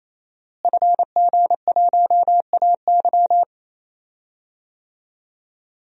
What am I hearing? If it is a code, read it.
Morse code, heard as a keyed tone, about 28 wpm, 700 Hz. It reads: FG1AY
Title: DXCC-100 Call Signs - 5 Characters - Repeated 3x - 28wpm